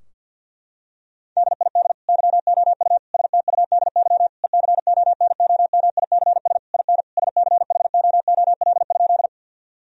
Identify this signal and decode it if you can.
Morse code, heard as a keyed tone, about 50 wpm, 700 Hz. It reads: DID YOU STUDY ECONOMICS IN SCHOOL?
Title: Sentences from Top 1000 Words - Review Part 3 - 50wpm